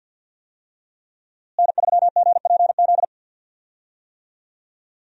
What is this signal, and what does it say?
Morse code, heard as a keyed tone, about 48 wpm, 700 Hz. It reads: N3GPZ
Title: USA Call Signs - 5 Characters - Repeated 2x - 50wpm